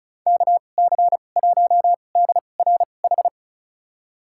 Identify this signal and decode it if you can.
Morse code, heard as a keyed tone, about 36 wpm, spaced as for 25 wpm, 700 Hz. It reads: KC1DRH
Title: USA Call Signs - 6 Characters - 2x Character Spacing - 35wpm